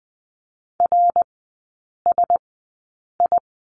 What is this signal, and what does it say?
Morse code, heard as a keyed tone, about 20 wpm, spaced as for 8 wpm, 700 Hz. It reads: RSI
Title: 074 Lesson 13e - Letter L - Three Characters 20wpm